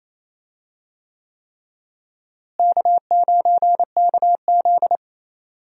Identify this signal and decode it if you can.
Morse code, heard as a keyed tone, about 28 wpm, 700 Hz. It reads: K9KZ
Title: USA Call Signs - 4 Characters - Repeated 3x - 28wpm